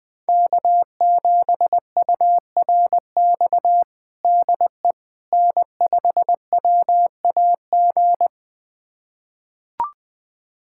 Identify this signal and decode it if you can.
Morse code, heard as a keyed tone, about 20 wpm, 700 Hz. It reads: K7URX DE N5WAG
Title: USA Callsign DE Callsign - 5 Characters - 20wpm